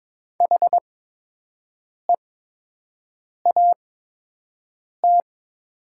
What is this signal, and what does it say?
Morse code, heard as a keyed tone, about 22 wpm, spaced as for 6 wpm, 700 Hz. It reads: HEAT
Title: 8x Character Spacing - Top 500 Words - 22wpm